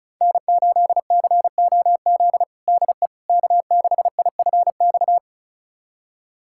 Morse code, heard as a keyed tone, about 34 wpm, 700 Hz. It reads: N8COZ DE K6IFX
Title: USA Callsign DE Callsign - 5 Characters - 35wpm